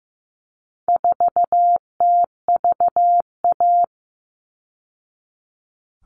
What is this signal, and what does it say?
Morse code, heard as a keyed tone, about 15 wpm, 700 Hz. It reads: 4TVA
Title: All Cut Numbers - 4-Digit 15wpm